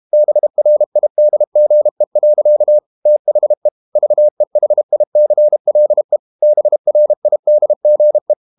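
Morse code, heard as a keyed tone, about 32 wpm, 600 Hz. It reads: BRIDGE. THE VEHICLE BRIDGE
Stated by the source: Morse code practice at 32 wpm from CW med Gnister, reading Ken Follett's novel Never in English.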